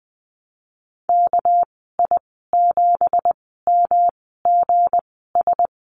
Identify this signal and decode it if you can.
Morse code, heard as a keyed tone, about 19 wpm, spaced as for 14 wpm, 700 Hz. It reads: KI7MGS
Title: 207 Lesson 33g - Letter j - Call Signs 20wpm